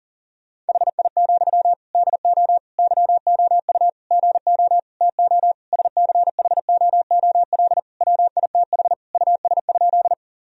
Morse code, heard as a keyed tone, about 37 wpm, 700 Hz. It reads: HI, DO YOU GO TO SCHOOL WITH US?
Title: Sentences from Top 1000 Words - Review Part 3 - 40wpm